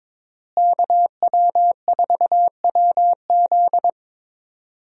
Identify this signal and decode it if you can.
Morse code, heard as a keyed tone, about 22 wpm, 700 Hz. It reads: KW4WZ
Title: USA Call Signs - 5 Characters - Repeated 2x - 22wpm